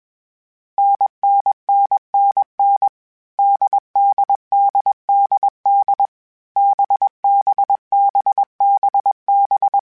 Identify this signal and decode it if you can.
Morse code, heard as a keyed tone, about 21 wpm, 800 Hz. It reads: NNNNN DDDDD BBBBB
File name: Short Runs / Short Runs-mp3s / NDB NDB6 @20